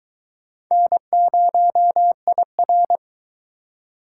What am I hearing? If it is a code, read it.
Morse code, heard as a keyed tone, about 22 wpm, 700 Hz. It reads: N0IR